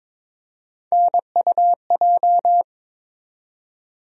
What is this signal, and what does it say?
Morse code, heard as a keyed tone, about 22 wpm, 700 Hz. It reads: NUJ
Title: USA Call Sign Postfixes - 3 Letters 22wpm